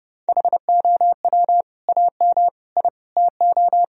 Morse code, heard as a keyed tone, about 29 wpm, 700 Hz. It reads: HOW AM I TO
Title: Sentences from Top 1000 Words - Review Part 4 - 30wpm